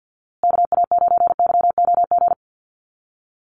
Morse code, heard as a keyed tone, about 50 wpm, 700 Hz. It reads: KA9YOG
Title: USA Call Signs Repeated 2x 50wpm